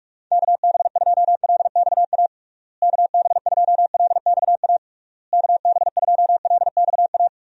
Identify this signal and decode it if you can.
Morse code, heard as a keyed tone, about 46 wpm, 700 Hz. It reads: KB2LXA KB2LXA KB2LXA
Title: USA Call Signs - 6 Characters - Repeated 3x - 45wpm